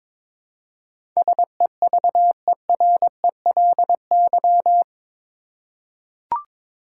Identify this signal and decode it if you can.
Morse code, heard as a keyed tone, about 22 wpm, 700 Hz. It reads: SEVERELY
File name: practice / 22wpm / Eight Letter Words - 22wpm